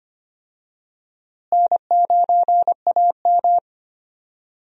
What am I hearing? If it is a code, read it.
Morse code, heard as a keyed tone, about 25 wpm, 700 Hz. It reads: N9AM
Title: USA Call Signs - 4 Characters - Repeated 2x - 25wpm